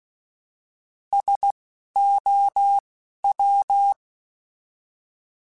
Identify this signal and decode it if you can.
Morse code, heard as a keyed tone, about 16 wpm, spaced as for 11 wpm, 800 Hz. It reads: SOW
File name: Alphabetical / Alphabetical mp3s / Ss @10